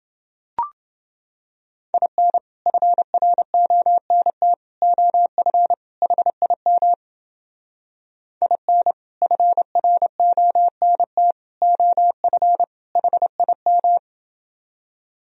Morse code, heard as a keyed tone, about 30 wpm, 700 Hz. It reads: IN FRONT OF HIM IN FRONT OF HIM